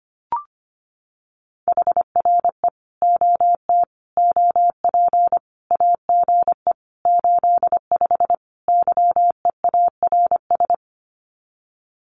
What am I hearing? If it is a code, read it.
Morse code, heard as a keyed tone, about 25 wpm, 700 Hz. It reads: HRE OT OP AGE 85 YEARS
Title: Common QSO Phrases 25wpm